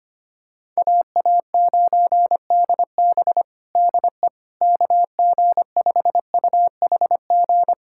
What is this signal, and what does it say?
Morse code, heard as a keyed tone, about 25 wpm, 700 Hz. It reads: AA9DB DE KG5UHG
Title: USA Callsign DE Callsign 25wpm